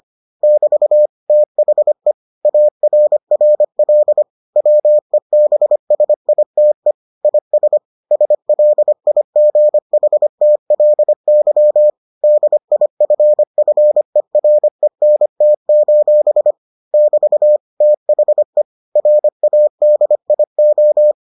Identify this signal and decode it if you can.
Morse code, heard as a keyed tone, about 25 wpm, 600 Hz. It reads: <BT> THE ARRL WEBSITE IS SLIGHTLY DIFFERENT: <BT> THE RADIO